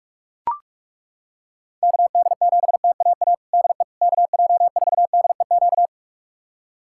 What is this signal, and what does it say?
Morse code, heard as a keyed tone, about 41 wpm, 700 Hz. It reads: KD7TAA DE KJ4DEQ
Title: USA Callsign DE Callsign 45wpm